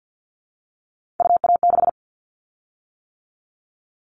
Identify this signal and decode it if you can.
Morse code, heard as a keyed tone, about 50 wpm, 700 Hz. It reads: UA6
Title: Callsign Prefixes for Top 25 DXCC - 50wpm